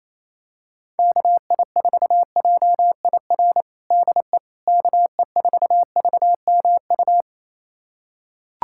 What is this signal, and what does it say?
Morse code, heard as a keyed tone, about 28 wpm, 700 Hz. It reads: KI4JIR DE KE4VMU
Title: USA Callsign DE Callsign - 6 Characters - 28wpm